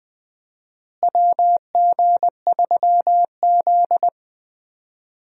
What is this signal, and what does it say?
Morse code, heard as a keyed tone, about 20 wpm, 700 Hz. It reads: WG3Z
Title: USA Call Signs - 4 Characters - Repeated 2x - 20wpm